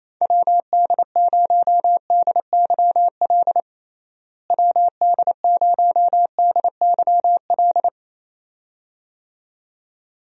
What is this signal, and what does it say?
Morse code, heard as a keyed tone, about 27 wpm, 700 Hz. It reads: WD0DYL WD0DYL
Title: USA Call Signs - 6 Characters - Repeated 2x - 28wpm